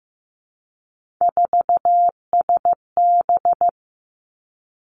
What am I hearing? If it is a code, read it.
Morse code, heard as a keyed tone, about 15 wpm, 700 Hz. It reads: 4SB
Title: Three Challenging Letter-Number Combinations 15wpm